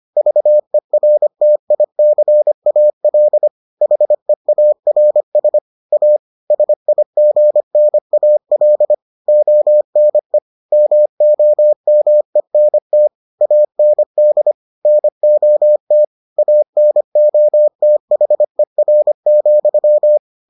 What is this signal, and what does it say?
Morse code, heard as a keyed tone, about 25 wpm, 600 Hz. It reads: VERTICAL HEARS A SIGNAL ONE MOMENT AND NOT ANOTHER,